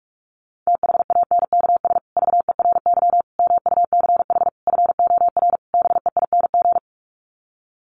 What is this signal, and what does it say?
Morse code, heard as a keyed tone, about 46 wpm, 700 Hz. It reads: THANKS VERY MUCH FOR BEING
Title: Sets of 5 Words Encore 45wpm